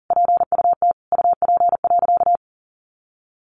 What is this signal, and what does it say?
Morse code, heard as a keyed tone, about 40 wpm, 700 Hz. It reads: PUT UP.
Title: A Sound Of Thunder 40wpm